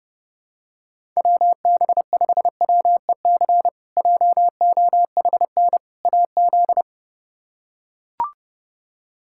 Morse code, heard as a keyed tone, about 30 wpm, 700 Hz. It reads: WB5WEC JOHN AZ
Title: K1USN SST QSO Practice - 30wpm